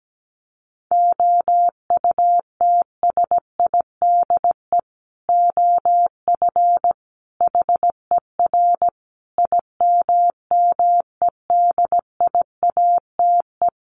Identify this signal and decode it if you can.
Morse code, heard as a keyed tone, about 17 wpm, 700 Hz. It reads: OUTSIDE OF HER IMMEDIATE
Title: Sentences from Top 1200 Words 17wpm